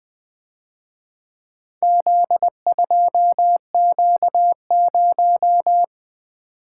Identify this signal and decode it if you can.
Morse code, heard as a keyed tone, about 24 wpm, 700 Hz. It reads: Z2Q0
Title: Speed-Racing - Four Letter-Number Combinations - 20wpm